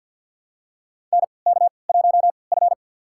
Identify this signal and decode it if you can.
Morse code, heard as a keyed tone, about 51 wpm, spaced as for 28 wpm, 700 Hz. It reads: NK1F
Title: USA Call Signs - 3x Character Spacing - 50wpm